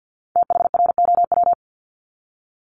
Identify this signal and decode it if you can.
Morse code, heard as a keyed tone, about 50 wpm, 700 Hz. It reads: THROW
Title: Multi-letter Phonemes Repeated 3x 50wpm